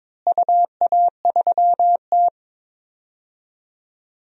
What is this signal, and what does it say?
Morse code, heard as a keyed tone, about 22 wpm, 700 Hz. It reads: UA3T
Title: DXCC-100 Call Signs - 4 Characters - Repeated 3x - 22wpm